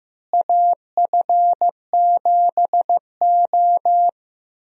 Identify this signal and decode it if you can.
Morse code, heard as a keyed tone, about 15 wpm, 700 Hz. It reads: AF7O
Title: USA Call Signs - 4 Characters - Repeated 2x - 15wpm